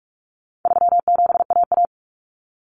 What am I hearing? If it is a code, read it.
Morse code, heard as a keyed tone, about 44 wpm, 700 Hz. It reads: 37AA
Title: Common Cut Numbers - 4-Digit 45wpm